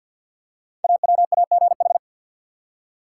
Morse code, heard as a keyed tone, about 47 wpm, 700 Hz. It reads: AWAGH
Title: Code Groups - 5 Letter:Numbers - 50wpm